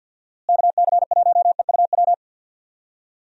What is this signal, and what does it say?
Morse code, heard as a keyed tone, about 50 wpm, 700 Hz. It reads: KC1EUW
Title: USA Call Signs - 6 Characters - Repeated 3x - 50wpm